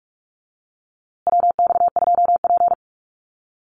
Morse code, heard as a keyed tone, about 44 wpm, 700 Hz. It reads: WX2P
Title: USA Call Signs Repeated 3x 45wpm